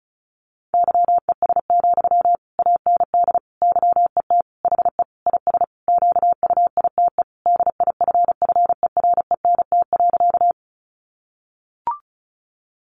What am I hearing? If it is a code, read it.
Morse code, heard as a keyed tone, about 34 wpm, 700 Hz. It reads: YES, AND YET HE IS QUITE DIFFERENT.
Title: Sentences from Top 700 Words 35wpm